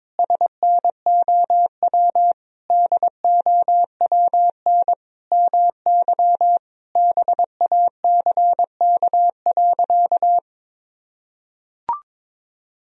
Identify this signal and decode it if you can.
Morse code, heard as a keyed tone, about 22 wpm, 700 Hz. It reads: SNOW DOWN MY BACK.
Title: Sentences from Top 1000 Words - Review Part 3 - 22wpm